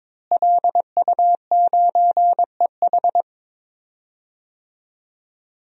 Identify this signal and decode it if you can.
Morse code, heard as a keyed tone, about 22 wpm, 700 Hz. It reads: LU9EH
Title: DXCC-100 Call Signs - 5 Characters - Repeated 3x - 22wpm